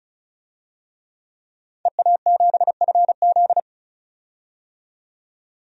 Morse code, heard as a keyed tone, about 34 wpm, 700 Hz. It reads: EA7FZ
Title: DXCC-100 Call Signs - 5 Characters - 35wpm